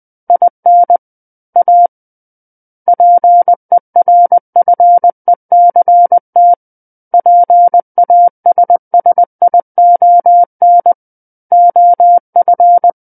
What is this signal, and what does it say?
Morse code, heard as a keyed tone, about 20 wpm, 700 Hz. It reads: IN A PERFECT PASSION OF
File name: 20 WPM